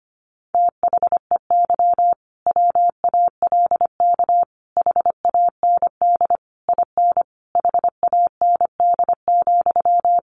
Morse code, heard as a keyed tone, about 25 wpm, 700 Hz. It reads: THEY WALK HAND IN HAND,
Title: Sentences from Top 1400 Words 25wpm